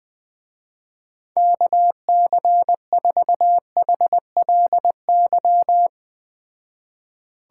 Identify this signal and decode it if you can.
Morse code, heard as a keyed tone, about 20 wpm, 700 Hz. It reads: KC4HLY